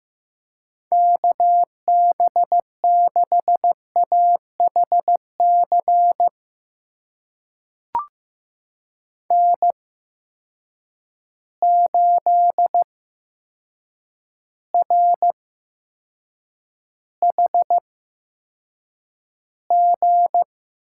Morse code, heard as a keyed tone, about 15 wpm, spaced as for 7 wpm, 700 Hz. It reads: KB6AHC N8RHG
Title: USA Call Signs - 8x Character Spacing - 15wpm